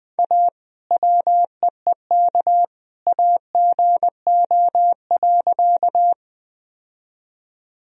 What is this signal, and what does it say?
Morse code, heard as a keyed tone, about 20 wpm, 700 Hz. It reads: A WEEK AGO.
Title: Sentences from Top 1000 Words - Review Part 3 - 20wpm